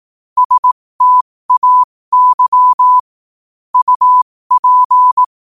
Morse code, heard as a keyed tone, about 18 wpm, spaced as for 15 wpm, 1000 Hz. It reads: STAY UP
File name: qotdfiles / 15wpm